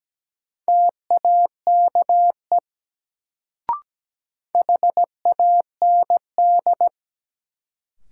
Morse code, heard as a keyed tone, about 17 wpm, 700 Hz. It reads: TAKE HAND
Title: Sentences from Top 200 Words - New Word Drill - 17wpm